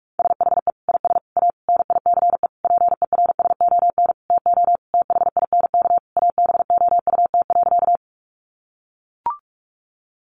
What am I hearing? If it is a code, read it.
Morse code, heard as a keyed tone, about 44 wpm, 700 Hz. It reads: SHE IS A NICE PERSON TO THINK ABOUT.
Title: Sentences from Top 1100 Words 45wpm